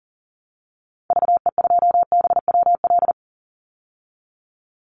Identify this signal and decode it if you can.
Morse code, heard as a keyed tone, about 40 wpm, 700 Hz. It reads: VE2BWL
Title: Call Signs Repeated 2x - 40wpm